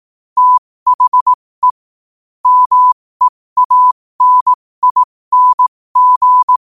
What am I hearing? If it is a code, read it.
Morse code, heard as a keyed tone, about 18 wpm, spaced as for 15 wpm, 1000 Hz. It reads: THE MEANING
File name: qotdfiles / 15wpm